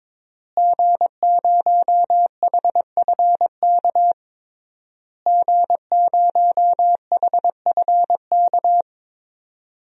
Morse code, heard as a keyed tone, about 21 wpm, 700 Hz. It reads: G0HFK G0HFK
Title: DXCC-100 Call Signs - 5 Characters - Repeated 3x - 22wpm